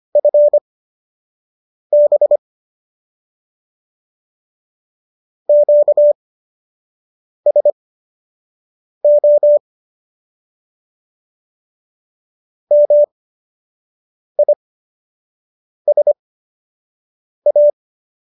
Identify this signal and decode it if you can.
Morse code, heard as a keyed tone, about 25 wpm, spaced as for 6 wpm, 590 Hz. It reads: FB QSO MISA